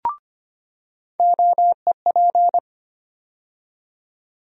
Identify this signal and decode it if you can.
Morse code, heard as a keyed tone, about 25 wpm, 700 Hz. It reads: OEP
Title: USA Call Sign Postfixes - 3 Letters 25wpm